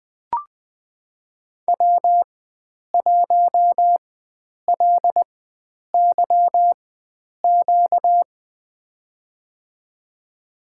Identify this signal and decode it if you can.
Morse code, heard as a keyed tone, about 20 wpm, spaced as for 9 wpm, 700 Hz. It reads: W1LYQ
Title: USA Call Signs - 5 Characters - 4x Character Spacing - 20wpm